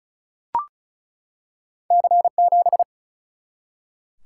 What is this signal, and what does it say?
Morse code, heard as a keyed tone, about 34 wpm, 700 Hz. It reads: C7
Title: Single-Letter Single-Number 35wpm